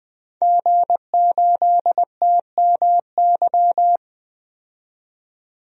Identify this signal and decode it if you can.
Morse code, heard as a keyed tone, about 20 wpm, 700 Hz. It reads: G8TMY
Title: DXCC-100 Call Signs - 5 Characters - 20wpm